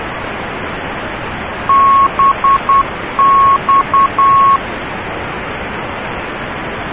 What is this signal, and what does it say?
Morse code, heard as a keyed tone, about 10 wpm, 1100 Hz, with noise. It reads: BX